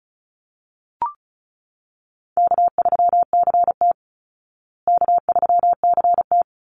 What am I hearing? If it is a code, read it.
Morse code, heard as a keyed tone, about 34 wpm, 700 Hz. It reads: K3CT K3CT
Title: Call Signs Repeated 2x - 35wpm